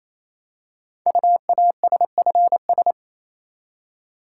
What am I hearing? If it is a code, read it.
Morse code, heard as a keyed tone, about 28 wpm, 700 Hz. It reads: UASFS
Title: Code Groups - 5 Letters - 28wpm